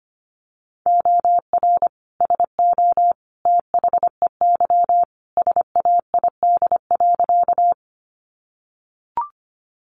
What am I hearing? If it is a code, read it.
Morse code, heard as a keyed tone, about 25 wpm, 700 Hz. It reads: OR SO THEY SAID.
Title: Sentences from Top 100 Words 25wpm